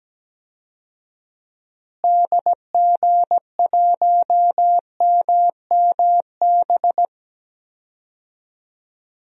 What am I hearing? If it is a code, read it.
Morse code, heard as a keyed tone, about 17 wpm, 700 Hz. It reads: DG1MMB
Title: DXCC-100 Call Signs - 6 Characters - Repeated 2x - 17wpm